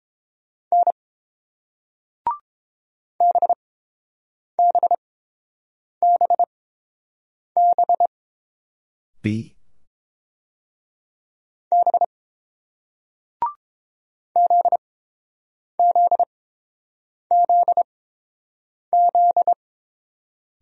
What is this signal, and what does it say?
Morse code, heard as a keyed tone, about 30 wpm, spaced as for 7 wpm, 700 Hz. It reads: N BBBB B ZZZZ